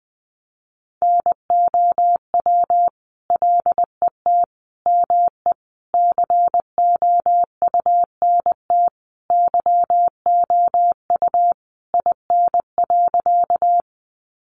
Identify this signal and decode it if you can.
Morse code, heard as a keyed tone, about 20 wpm, 700 Hz. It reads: NOW LET ME COUNT YOU IN.
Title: Sentences from Top 1500 Words 20wpm